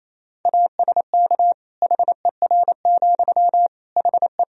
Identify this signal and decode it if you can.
Morse code, heard as a keyed tone, about 28 wpm, 700 Hz. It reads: ASK HER, HE
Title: Sentences from Top 200 Words and Abbreviations 28wpm